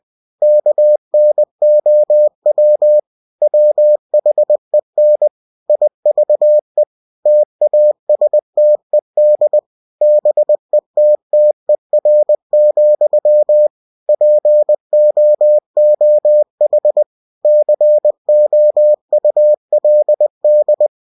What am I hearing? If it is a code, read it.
Morse code, heard as a keyed tone, about 20 wpm, 600 Hz. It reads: KNOW WHEN IVE TASTED BETTER, POOH COULD
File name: cw-books / winnie-the-pooh / winnie-20wpm-page-0040